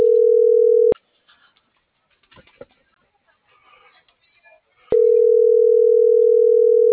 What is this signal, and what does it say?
Ringback tone (North America), 440 + 480 Hz.